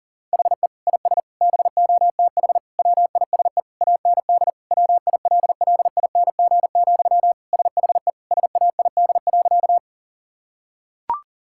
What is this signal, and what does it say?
Morse code, heard as a keyed tone, about 37 wpm, 700 Hz. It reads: HE IS BOTH WISE AND WILLING, SHE SAID.
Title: Sentences from Top 1600 Words 40wpm